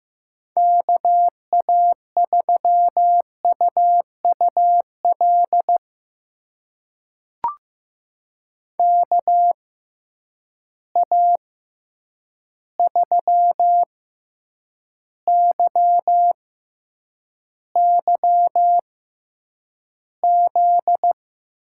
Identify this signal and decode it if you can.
Morse code, heard as a keyed tone, about 15 wpm, spaced as for 8 wpm, 700 Hz. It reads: KA3UUL KA3YYZ
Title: USA Call Signs - 6 Characters - 6x Character Spacing - 15wpm